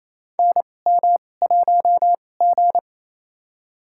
Morse code, heard as a keyed tone, about 28 wpm, spaced as for 20 wpm, 700 Hz. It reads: NM1G
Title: USA Call Signs - 4 Characters - 2x Character Spacing - 28wpm